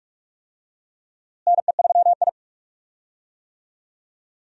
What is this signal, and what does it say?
Morse code, heard as a keyed tone, about 46 wpm, 700 Hz. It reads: NE3I
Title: USA Call Signs - 4 Characters - Repeated 3x - 45wpm